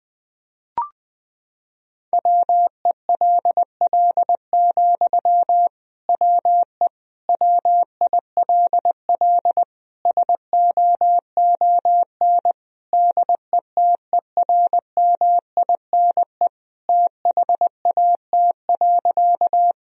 Morse code, heard as a keyed tone, about 19 wpm, 700 Hz. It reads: WELL, WE WILL SOON DETERMINE THAT.
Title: Sentences from Top 1000 Words - Review Part 4 - 20wpm